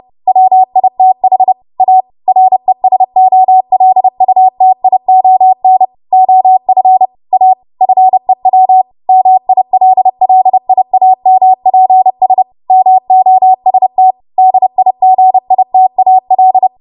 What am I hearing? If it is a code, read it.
Morse code, heard as a keyed tone, about 30 wpm, 750 Hz. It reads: WITH A RESOLUTION OF A FEW MILLIAMPS MOST DIGITAL
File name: Archive / 30 WPM / 30WPM